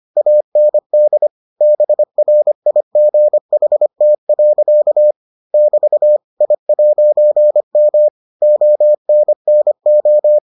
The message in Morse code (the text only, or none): AND BRIGHT. <BT> I'M ONNO